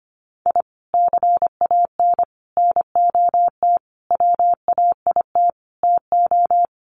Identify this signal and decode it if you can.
Morse code, heard as a keyed tone, about 25 wpm, 700 Hz. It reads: I CAN NOT WAIT TO
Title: Sets of 5 Words Encore 25wpm